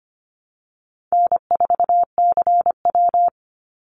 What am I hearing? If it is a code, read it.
Morse code, heard as a keyed tone, about 25 wpm, 700 Hz. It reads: N4CW
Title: Call Signs - 25wpm